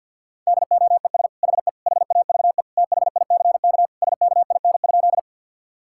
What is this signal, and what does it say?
Morse code, heard as a keyed tone, about 50 wpm, 700 Hz. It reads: DOES HE HAVE THICK SKIN?
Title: Sentences from Top 1600 Words 50wpm